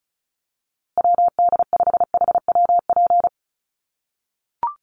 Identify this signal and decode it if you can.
Morse code, heard as a keyed tone, about 34 wpm, 700 Hz. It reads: WD5HWP
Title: USA Call Signs Repeated 2x 35wpm